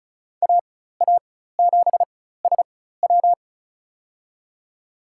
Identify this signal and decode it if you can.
Morse code, heard as a keyed tone, about 34 wpm, spaced as for 16 wpm, 700 Hz. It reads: AA7SW
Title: USA Call Signs - 5 Characters - 4x Character Spacing - 35wpm